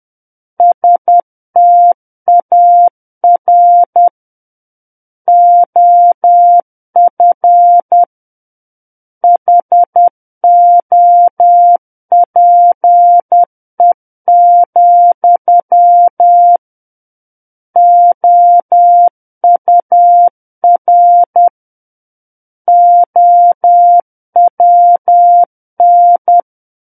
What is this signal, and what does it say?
Morse code, heard as a keyed tone, about 10 wpm, 700 Hz. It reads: STAR OF HOPE, OUR OWN
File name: Book One / 01-Chapter 1 - 10 WPM